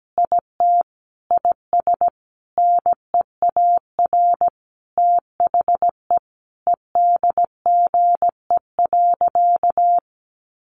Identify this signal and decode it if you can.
Morse code, heard as a keyed tone, about 17 wpm, 700 Hz. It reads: IT IS NEAR THE EDGE.
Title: Additional Easy Phrases 17wpm